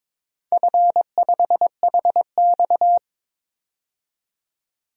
Morse code, heard as a keyed tone, about 21 wpm, 700 Hz. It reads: F5HX
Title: DXCC-100 Call Signs - 4 Characters - 22wpm